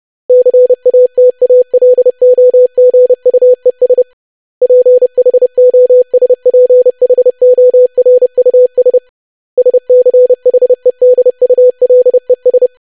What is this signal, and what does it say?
Morse code, heard as a keyed tone, about 30 wpm, 500 Hz. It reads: CATALOGUES PHOSPHORUS SCHEDULES
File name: lesson / web / longwords-25wpm